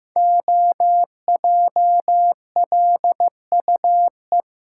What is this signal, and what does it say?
Morse code, heard as a keyed tone, about 15 wpm, 700 Hz. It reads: OJLUE
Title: Code Groups - 5 Letter:Numbers - 15wpm